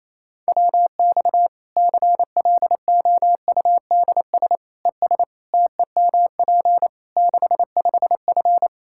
Morse code, heard as a keyed tone, about 28 wpm, 700 Hz. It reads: WX CLOUDS ES TEMP 65F
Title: Practice QSO - Exchange 2 - Info - 28wpm